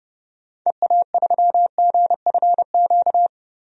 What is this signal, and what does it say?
Morse code, heard as a keyed tone, about 32 wpm, 700 Hz. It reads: EA3GFQ
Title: DXCC-100 Call Signs - 6 Characters - 30wpm